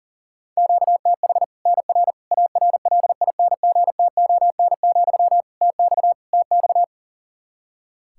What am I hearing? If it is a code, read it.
Morse code, heard as a keyed tone, about 37 wpm, 700 Hz. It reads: QTH NR ARLINGTON, TX TX
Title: Practice QSO - Exchange 1 - Beginning - 40wpm